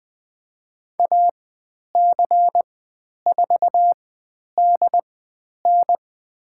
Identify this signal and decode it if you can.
Morse code, heard as a keyed tone, about 20 wpm, spaced as for 10 wpm, 700 Hz. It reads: AC4DN
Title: 102 Lesson 17g - Letter u - Call Signs 20wpm